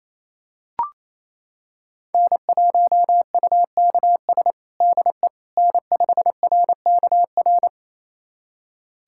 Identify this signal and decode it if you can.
Morse code, heard as a keyed tone, about 28 wpm, 700 Hz. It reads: N1UKS DE N5RKR
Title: USA Callsign DE Callsign - 5 Characters - 28wpm